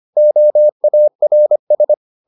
Morse code, heard as a keyed tone, about 25 wpm, 600 Hz. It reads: OARS